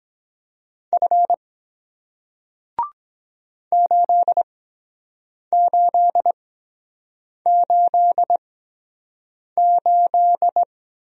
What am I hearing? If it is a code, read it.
Morse code, heard as a keyed tone, about 23 wpm, spaced as for 7 wpm, 700 Hz. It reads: F 8888